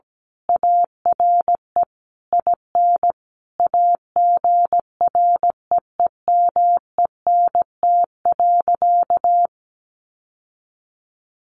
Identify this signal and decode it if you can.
Morse code, heard as a keyed tone, about 17 wpm, 700 Hz. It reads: ARE IN AGREEMENT.
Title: Sentences from Top 1500 Words 17wpm